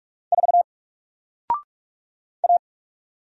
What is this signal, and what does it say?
Morse code, heard as a keyed tone, about 45 wpm, 700 Hz. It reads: 4 A
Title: Single Character (Letter:Number:Punctuation:Prosign) 45wpm